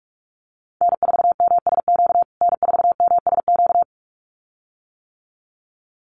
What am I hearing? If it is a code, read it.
Morse code, heard as a keyed tone, about 44 wpm, 690 Hz. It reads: N4MSQ N4MSQ
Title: USA Call Signs Repeated 2x 45wpm